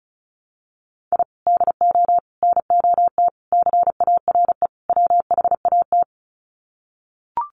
Morse code, heard as a keyed tone, about 34 wpm, 700 Hz. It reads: I DO NOT CARE WHAT
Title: Sets of 5 Words 35wpm